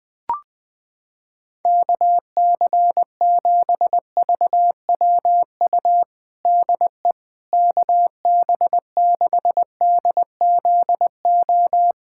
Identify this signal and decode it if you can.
Morse code, heard as a keyed tone, about 20 wpm, 700 Hz. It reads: KC7VWU DE KB6DZO
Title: USA Callsign DE Callsign 20wpm